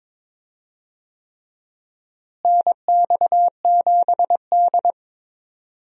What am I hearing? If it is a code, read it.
Morse code, heard as a keyed tone, about 23 wpm, 700 Hz. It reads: NX7D
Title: USA Call Signs - 4 Characters - Repeated 3x - 22wpm